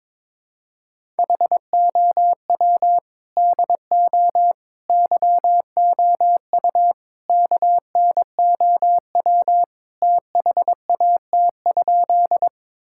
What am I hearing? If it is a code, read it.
Morse code, heard as a keyed tone, about 22 wpm, 700 Hz. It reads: HOW DO YOU KNOW THAT?
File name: practice / 22wpm / Sentences with 4-Word Phrases - 22wpm